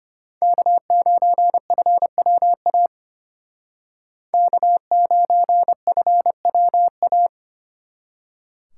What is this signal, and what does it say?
Morse code, heard as a keyed tone, about 31 wpm, 700 Hz. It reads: K9FWA K9FWA
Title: Speed Racing - USA 1x3 Call Signs 25wpm